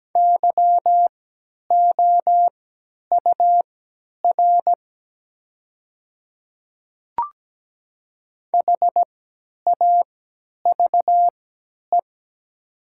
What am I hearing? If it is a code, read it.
Morse code, heard as a keyed tone, about 17 wpm, spaced as for 10 wpm, 700 Hz. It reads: YOUR HAVE
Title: 3x Character Spacing - Top 100 Words - 17wpm